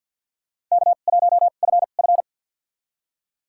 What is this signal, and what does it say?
Morse code, heard as a keyed tone, about 52 wpm, spaced as for 36 wpm, 700 Hz. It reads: K1FF
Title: USA Call Signs - 4 Characters - 2x Character Spacing - 50wpm